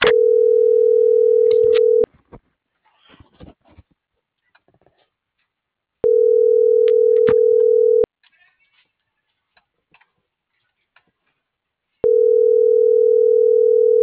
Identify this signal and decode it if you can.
Ringback tone (North America), 440 + 480 Hz.